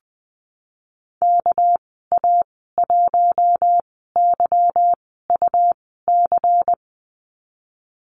Morse code, heard as a keyed tone, about 19 wpm, spaced as for 14 wpm, 700 Hz. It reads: KA1YUC
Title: 207 Lesson 33g - Letter j - Call Signs 20wpm